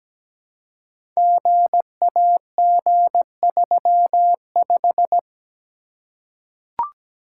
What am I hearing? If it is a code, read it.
Morse code, heard as a keyed tone, about 17 wpm, 700 Hz. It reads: GAG35
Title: Code Groups - 5 Letter:Numbers - 17wpm